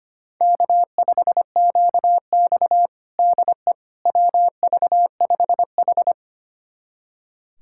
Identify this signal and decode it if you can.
Morse code, heard as a keyed tone, about 25 wpm, 700 Hz. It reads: K5QX DE WV5H
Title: USA Callsign DE Callsign - 4 Characters - 25wpm